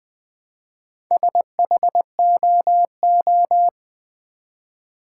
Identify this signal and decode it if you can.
Morse code, heard as a keyed tone, about 20 wpm, 700 Hz. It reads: SHOO
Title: Build A Word - 1 Letter At A Time - 5-Letter Words - 20wpm